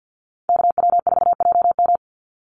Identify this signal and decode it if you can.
Morse code, heard as a keyed tone, about 48 wpm, 700 Hz. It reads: KW4JM
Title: NA SOTA Activator Callsigns 50wpm